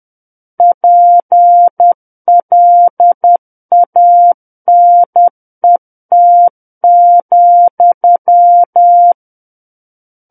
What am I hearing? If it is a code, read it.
Morse code, heard as a keyed tone, about 10 wpm, 700 Hz. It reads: PLANET,
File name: Book One / 01-Chapter 1 - 10 WPM